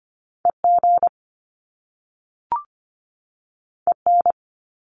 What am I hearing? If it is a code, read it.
Morse code, heard as a keyed tone, about 25 wpm, 700 Hz. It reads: EG EN
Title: Top Two Letter Combinations in Words 25wpm